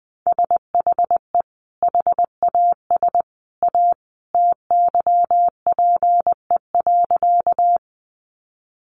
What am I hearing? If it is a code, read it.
Morse code, heard as a keyed tone, about 20 wpm, 700 Hz. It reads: SHE HAS A TYPE.
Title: Sentences from Top 600 Words 20wpm